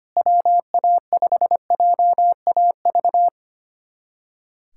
Morse code, heard as a keyed tone, about 25 wpm, 700 Hz. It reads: WA5JAV
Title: USA Callsign DE Callsign 25wpm